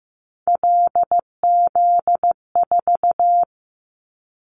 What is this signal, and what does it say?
Morse code, heard as a keyed tone, about 15 wpm, 700 Hz. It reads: LZ4
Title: Callsign Prefixes for Top 50 DXCC - 15wpm